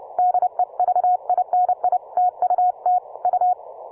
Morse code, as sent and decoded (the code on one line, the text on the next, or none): -.. ...- .. -. .. / - ..- - / ..-
DEVINI TUT U